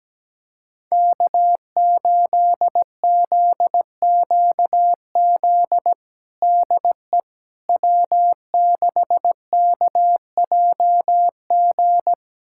Morse code, heard as a keyed tone, about 17 wpm, 700 Hz. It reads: K8ZQZ DE W6KJG
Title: USA Callsign DE Callsign - 5 Characters - 17wpm